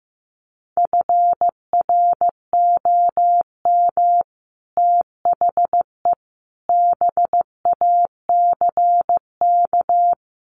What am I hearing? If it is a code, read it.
Morse code, heard as a keyed tone, about 15 wpm, 700 Hz. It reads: FROM THE BACK
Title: Sets of 5 Words Encore 15wpm